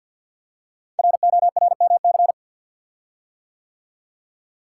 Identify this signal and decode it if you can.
Morse code, heard as a keyed tone, about 49 wpm, 700 Hz. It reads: RORMC
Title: Code Groups - 5 Letters - 50wpm